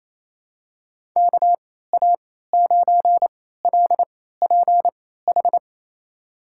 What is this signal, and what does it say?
Morse code, heard as a keyed tone, about 28 wpm, spaced as for 16 wpm, 700 Hz. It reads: KA9LPH
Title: USA Call Signs - 3x Character Spacing - 28wpm